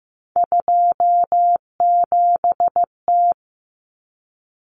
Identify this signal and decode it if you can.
Morse code, heard as a keyed tone, about 15 wpm, 700 Hz. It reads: 27T
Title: Common Cut Numbers - 3-Digit 15wpm